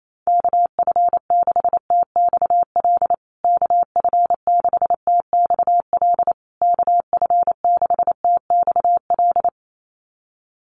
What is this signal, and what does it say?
Morse code, heard as a keyed tone, about 28 wpm, 700 Hz. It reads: KF6TXL KF6TXL KF6TXL
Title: USA Call Signs Repeated 3x 28wpm